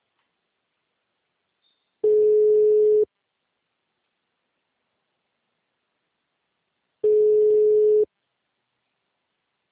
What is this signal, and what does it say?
Ringback tone (Europe, 425 Hz).